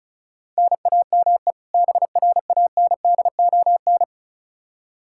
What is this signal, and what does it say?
Morse code, heard as a keyed tone, about 36 wpm, 700 Hz. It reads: NAME BRANDON
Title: K1USN SST QSO Practice 35wpm